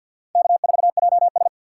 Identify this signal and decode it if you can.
Morse code, heard as a keyed tone, about 51 wpm, 700 Hz. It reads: K4JS
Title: USA Call Signs - 3x Character Spacing - 50wpm